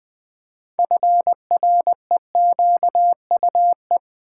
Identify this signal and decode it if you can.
Morse code, heard as a keyed tone, about 20 wpm, 690 Hz. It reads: FREQUE
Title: Build A Word - 3 Letters At A Time - 8-Letter Words - 20wpm